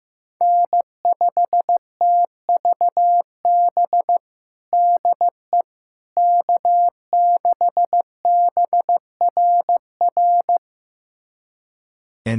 Morse code, heard as a keyed tone, about 15 wpm, 700 Hz. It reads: N5TVB DE K6BRR
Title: USA Callsign DE Callsign - 5 Characters - 15wpm